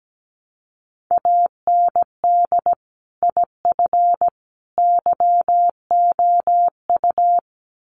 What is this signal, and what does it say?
Morse code, heard as a keyed tone, about 17 wpm, 700 Hz. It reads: AND IF YOU
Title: Sets of 3 Words 17wpm